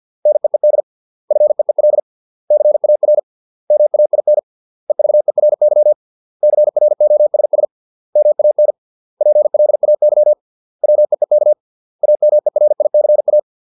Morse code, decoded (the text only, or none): NEED FEEL CAR MAIN EVERY CROSS MAN PLAY WEEK AMERICA